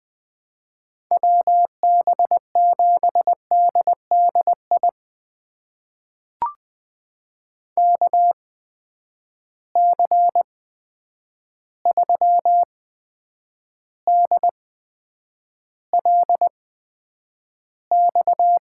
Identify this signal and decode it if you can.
Morse code, heard as a keyed tone, about 20 wpm, spaced as for 9 wpm, 700 Hz. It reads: WB7DDI KC3DLX
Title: USA Call Signs - 6 Characters - 8x Character Spacing - 20wpm